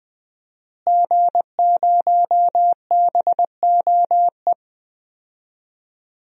Morse code, heard as a keyed tone, about 20 wpm, 700 Hz. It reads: G0BOE
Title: DXCC-100 Call Signs - 5 Characters - Repeated 2x - 20wpm